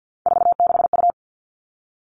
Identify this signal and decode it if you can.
Morse code, heard as a keyed tone, about 50 wpm, 700 Hz. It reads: 46U
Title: Three Challenging Letter-Number Combinations 50wpm